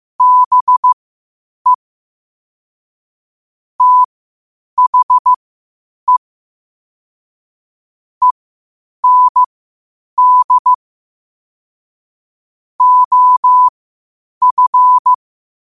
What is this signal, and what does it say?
Morse code, heard as a keyed tone, about 15 wpm, spaced as for 8 wpm, 1000 Hz. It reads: BE THE END OF
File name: qotdfiles / 8wpm